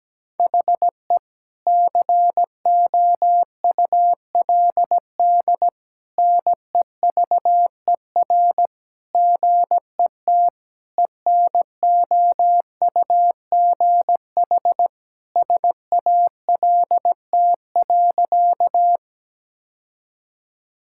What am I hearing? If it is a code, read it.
Morse code, heard as a keyed tone, about 17 wpm, 700 Hz. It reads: HE COULD NEVER GET ENOUGH SALT.
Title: Sentences from Top 1000 Words - Review Part 5 - 17wpm